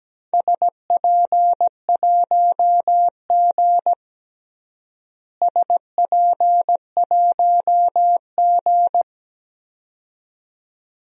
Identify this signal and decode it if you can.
Morse code, heard as a keyed tone, about 17 wpm, 700 Hz. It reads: SP1G SP1G
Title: DXCC-100 Call Signs - 4 Characters - Repeated 2x - 17wpm